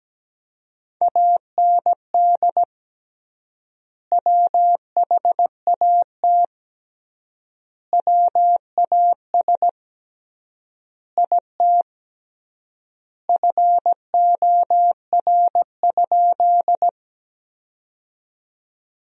Morse code, heard as a keyed tone, about 17 wpm, 700 Hz. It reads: AND WHAT WAS IT FOR?
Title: 3x Word-Spacing - Sentences from Top 100 Words - 17wpm